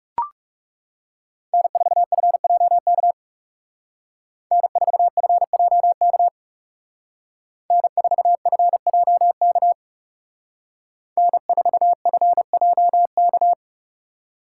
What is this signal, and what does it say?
Morse code, heard as a keyed tone, about 39 wpm, 700 Hz. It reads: N4FJK N4FJK N4FJK N4FJK